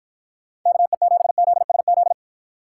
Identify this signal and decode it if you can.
Morse code, heard as a keyed tone, about 52 wpm, 700 Hz. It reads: KE7ZSZ